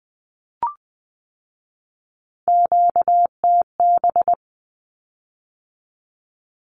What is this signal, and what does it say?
Morse code, heard as a keyed tone, about 20 wpm, 700 Hz. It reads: QTB
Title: Q-Codes with Spoken Meaning - 20wpm